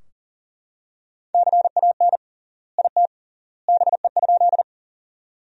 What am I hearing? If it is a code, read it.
Morse code, heard as a keyed tone, about 40 wpm, 700 Hz. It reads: CAN IT BE?
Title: 3x Word-Spacing - Sentences from Top 100 Words - 40wpm